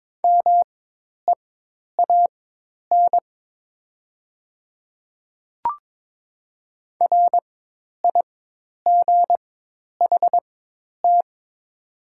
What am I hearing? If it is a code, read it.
Morse code, heard as a keyed tone, about 22 wpm, spaced as for 10 wpm, 700 Hz. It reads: MEAN RIGHT
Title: 4x Character Spacing - Top 200 Words - 22wpm